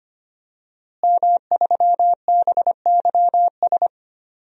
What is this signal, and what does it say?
Morse code, heard as a keyed tone, about 25 wpm, 700 Hz. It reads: M3BYS
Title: DXCC-100 Call Signs - 5 Characters - Repeated 2x - 25wpm